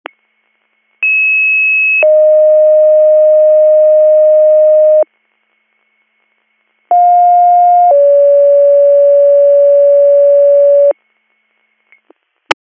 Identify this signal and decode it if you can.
Two-tone page, heard as a one-second tone, then a three-second tone, about 2600 then 620 Hz, about 710 then 570 Hz. It is heard from 1.0 to 10.9 s.